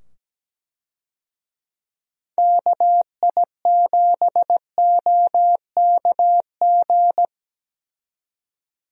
Morse code, heard as a keyed tone, about 17 wpm, 700 Hz. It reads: KI7OKG
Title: USA Call Signs - 6 Characters - 17wpm